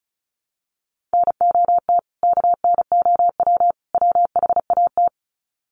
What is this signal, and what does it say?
Morse code, heard as a keyed tone, about 34 wpm, 700 Hz. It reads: NOT KNOW WHAT
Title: Sets of 3 Words 35wpm